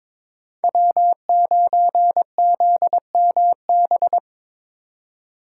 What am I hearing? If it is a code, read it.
Morse code, heard as a keyed tone, about 23 wpm, 700 Hz. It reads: W9ZMB
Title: USA Call Signs - 4x Character Spacing - 22wpm